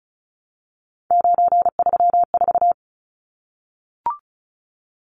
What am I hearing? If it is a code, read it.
Morse code, heard as a keyed tone, about 34 wpm, 700 Hz. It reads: 934
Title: Numbers - 3-Digit 35wpm